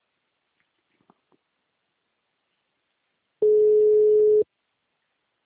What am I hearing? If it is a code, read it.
Ringback tone (Europe, 425 Hz).